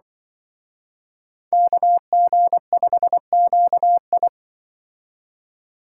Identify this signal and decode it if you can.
Morse code, heard as a keyed tone, about 24 wpm, 700 Hz. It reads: KG5QI
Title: Speed Racing - USA 2x2 Call Signs 20wpm